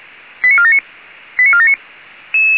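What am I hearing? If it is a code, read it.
Five-tone selective call (ZVEI), heard as quick runs of five tones, tones of about 70 ms. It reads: 89479 89479
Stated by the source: Five-tone ZVEI selective call received by SDR.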